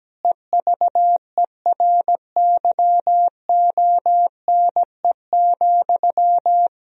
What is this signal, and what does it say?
Morse code, heard as a keyed tone, about 17 wpm, 700 Hz. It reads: EVERYONE,
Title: Sentences from Top 1000 Words - Review Part 3 - 17wpm